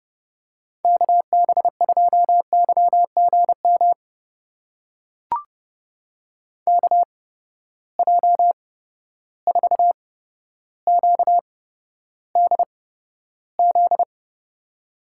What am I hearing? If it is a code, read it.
Morse code, heard as a keyed tone, about 30 wpm, spaced as for 13 wpm, 700 Hz. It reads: KB2YGM KJ4QDZ
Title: USA Call Signs - 6 Characters - 8x Character Spacing - 30wpm